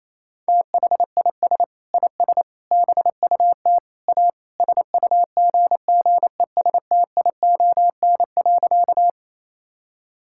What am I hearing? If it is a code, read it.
Morse code, heard as a keyed tone, about 28 wpm, 700 Hz. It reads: THIS IS BUT A SUGGESTION.
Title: Sentences from Top 1600 Words 28wpm